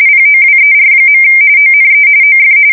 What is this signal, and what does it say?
RTTY, about 45.45 baud. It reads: POWER. ONE OF
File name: fsk170 / USS-New-Jersey